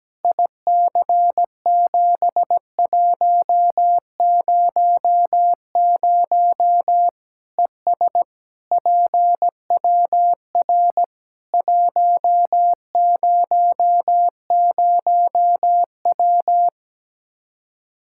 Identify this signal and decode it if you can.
Morse code, heard as a keyed tone, about 17 wpm, 700 Hz. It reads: IC7100 ES PWR 100W
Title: Practice QSO - Exchange 2 - Info - 17wpm